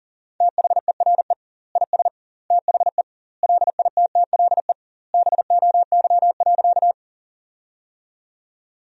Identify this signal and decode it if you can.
Morse code, heard as a keyed tone, about 40 wpm, 700 Hz. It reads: THERE IS THE LITTLE BOY.
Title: 2x Word-Spacing - Sentences from Top 200 Words - 40wpm